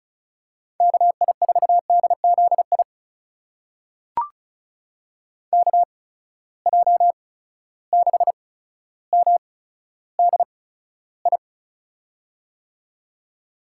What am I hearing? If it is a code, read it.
Morse code, heard as a keyed tone, about 36 wpm, spaced as for 15 wpm, 700 Hz. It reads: KI4DZI KJ6MDI
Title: USA Call Signs - 6 Characters - 8x Character Spacing - 35wpm